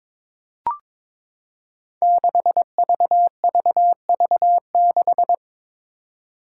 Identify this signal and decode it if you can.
Morse code, heard as a keyed tone, about 22 wpm, 700 Hz. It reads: 6VVV6
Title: Five Challenging Letter-Number Combinations 22wpm